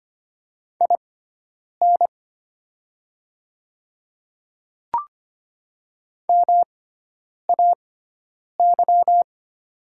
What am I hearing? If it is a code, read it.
Morse code, heard as a keyed tone, about 25 wpm, spaced as for 9 wpm, 700 Hz. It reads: IN MAY
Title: 6x Character Spacing - Top 100 Words - 25wpm